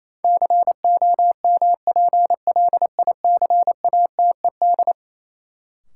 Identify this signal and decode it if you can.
Morse code, heard as a keyed tone, about 28 wpm, 700 Hz. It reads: COMPLICATED
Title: Sentences from Top 1200 Words - New Word Drill - 28wpm